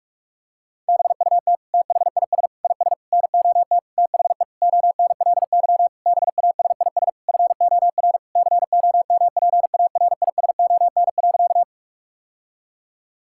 Morse code, heard as a keyed tone, about 46 wpm, 700 Hz. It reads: BUT THIS IS NOT THE ONLY BASIS FOR COMPARISON.
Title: Sentences from Top 1600 Words 45wpm